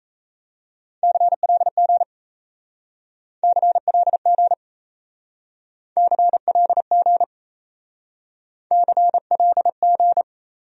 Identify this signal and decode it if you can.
Morse code, heard as a keyed tone, about 34 wpm, 700 Hz. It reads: CLG CLG CLG CLG
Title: Speed-Racing Abbreviations - 28wpm